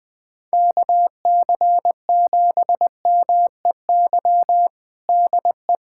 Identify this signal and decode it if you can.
Morse code, heard as a keyed tone, about 20 wpm, 700 Hz. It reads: KC7MEY DE
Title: USA Callsign DE Callsign 20wpm